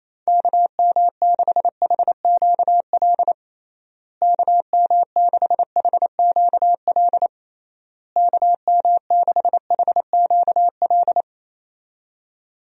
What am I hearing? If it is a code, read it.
Morse code, heard as a keyed tone, about 28 wpm, 700 Hz. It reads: KM6HQL KM6HQL KM6HQL
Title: USA Call Signs - 6 Characters - Repeated 3x - 28wpm